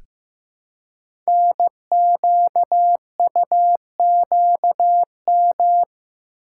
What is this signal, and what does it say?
Morse code, heard as a keyed tone, about 15 wpm, 700 Hz. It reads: NQUQM
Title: Code Groups - 5 Letters - 15wpm